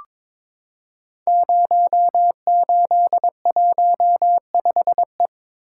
Morse code, heard as a keyed tone, about 23 wpm, 700 Hz. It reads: 0815E
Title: Code Groups - 5 Letter:Numbers - 22wpm